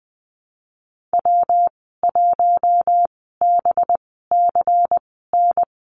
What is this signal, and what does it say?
Morse code, heard as a keyed tone, about 20 wpm, spaced as for 15 wpm, 700 Hz. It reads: W1BCN
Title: 186 Lesson 30g - Letter b - Call Signs 20wpm